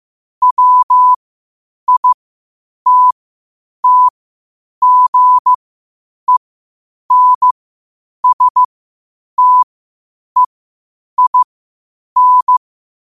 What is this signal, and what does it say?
Morse code, heard as a keyed tone, about 15 wpm, spaced as for 8 wpm, 1000 Hz. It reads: WITTGENSTEIN
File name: qotdfiles / 8wpm